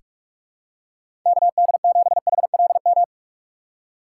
Morse code, heard as a keyed tone, about 46 wpm, 700 Hz. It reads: KD7HLM
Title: USA Call Signs - 6 Characters - 3x Character Spacing - 45wpm